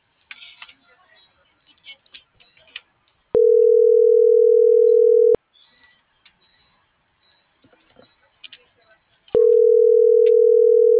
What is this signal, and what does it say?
Ringback tone (North America), 440 + 480 Hz.
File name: CENTRAL MAYAB / out fernanda.pamela